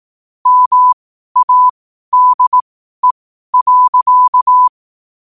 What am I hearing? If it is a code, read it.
Morse code, heard as a keyed tone, about 18 wpm, spaced as for 12 wpm, 1000 Hz. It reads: MADE.
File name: qotdfiles / 12wpm